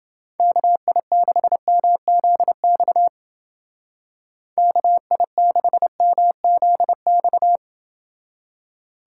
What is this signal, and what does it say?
Morse code, heard as a keyed tone, about 28 wpm, 700 Hz. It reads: KI6MZX KI6MZX